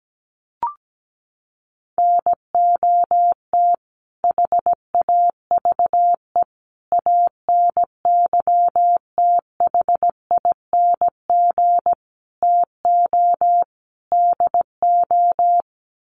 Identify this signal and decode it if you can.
Morse code, heard as a keyed tone, about 17 wpm, 700 Hz. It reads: NOT HAVE ANYTHING TO DO
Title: Sets of 5 Words Encore 17wpm